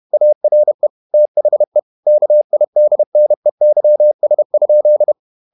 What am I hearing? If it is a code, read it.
Morse code, heard as a keyed tone, about 31 wpm, 600 Hz. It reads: ARE THE KIDNEYS?
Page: Never 31wpm | CW med Gnister